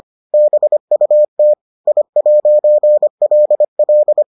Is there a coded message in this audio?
Yes: Morse code from 0.3 to 4.2 s, heard as a keyed tone, about 25 wpm, 600 Hz. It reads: BUT I'LL